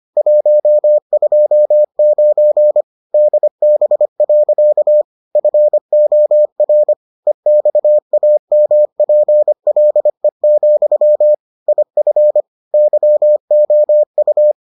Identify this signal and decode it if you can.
Morse code, heard as a keyed tone, about 25 wpm, 600 Hz. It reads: 129 DB. FOR EXAMPLE, IF YOU